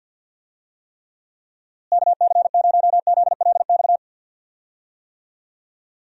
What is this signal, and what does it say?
Morse code, heard as a keyed tone, about 46 wpm, 700 Hz. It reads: KC0ZLX
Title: USA Call Signs - 6 Characters - Repeated 2x - 50wpm